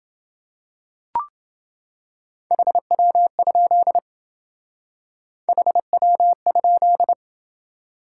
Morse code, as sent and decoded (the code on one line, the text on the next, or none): .... .-- ..--.. / .... .-- ..--..
HW? HW?